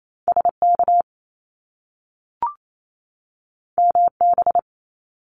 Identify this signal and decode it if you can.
Morse code, heard as a keyed tone, about 28 wpm, 700 Hz. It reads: SK MB
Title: Canadian Provinces and Territories 28wpm